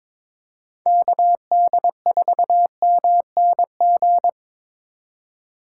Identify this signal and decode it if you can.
Morse code, heard as a keyed tone, about 25 wpm, 700 Hz. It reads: KD4MNG
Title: Speed Racing - USA 2x3 Call Signs 22wpm